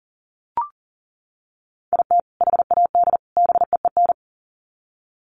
Morse code, heard as a keyed tone, about 40 wpm, 700 Hz. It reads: IT HAD BEEN
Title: Sets of 3 Words 40wpm